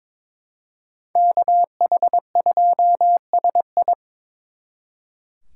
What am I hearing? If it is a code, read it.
Morse code, heard as a keyed tone, about 24 wpm, 700 Hz. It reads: KH2SI
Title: Speed Racing - USA 2x2 Call Signs 22wpm